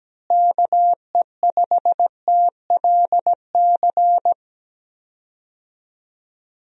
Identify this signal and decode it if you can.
Morse code, heard as a keyed tone, about 17 wpm, 700 Hz. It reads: KE5TLC
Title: USA Call Signs - 6 Characters - Repeated 3x - 17wpm